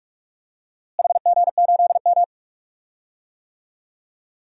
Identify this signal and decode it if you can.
Morse code, heard as a keyed tone, about 44 wpm, 700 Hz. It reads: HG8M
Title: DXCC-100 Call Signs - 4 Characters - Repeated 3x - 45wpm